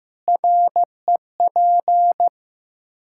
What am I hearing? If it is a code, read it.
Morse code, heard as a keyed tone, about 16 wpm, 700 Hz. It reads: REP
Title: Practice QSO - Exchange 3 - More Info - 15wpm